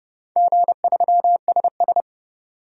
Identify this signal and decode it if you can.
Morse code, heard as a keyed tone, about 31 wpm, 700 Hz. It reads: G3SS
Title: DXCC-100 Call Signs - 4 Characters - 30wpm